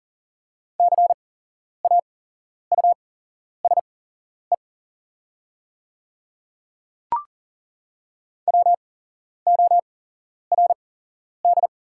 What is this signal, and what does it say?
Morse code, heard as a keyed tone, about 38 wpm, spaced as for 11 wpm, 690 Hz. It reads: CAUSE WORD
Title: 8x Character Spacing - Top 300 Words - 40wpm